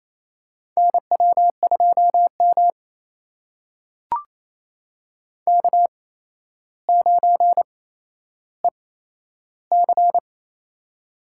Morse code, heard as a keyed tone, about 28 wpm, spaced as for 12 wpm, 700 Hz. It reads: NW2M K9EC
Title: USA Call Signs - 4 Characters - 8x Character Spacing - 28wpm